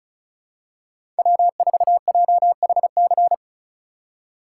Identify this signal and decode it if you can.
Morse code, heard as a keyed tone, about 36 wpm, 700 Hz. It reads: W4JHC
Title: USA Call Signs - 5 Characters - 3x Character Spacing - 35wpm